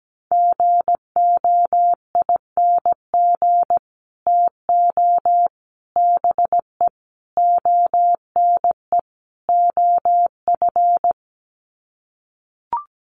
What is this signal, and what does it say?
Morse code, heard as a keyed tone, about 17 wpm, 700 Hz. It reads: GOING TO BE ONE OF
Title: Sets of 5 Words Encore2 17wpm